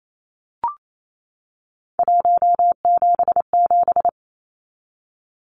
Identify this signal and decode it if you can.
Morse code, heard as a keyed tone, about 28 wpm, 700 Hz. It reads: 177
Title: Numbers - 3-Digit 28wpm